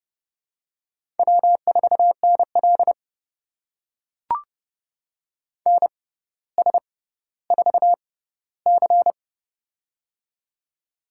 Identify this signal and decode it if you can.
Morse code, heard as a keyed tone, about 30 wpm, spaced as for 15 wpm, 700 Hz. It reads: W4NL NS4C
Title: USA Call Signs - 4 Characters - 6x Character Spacing - 30wpm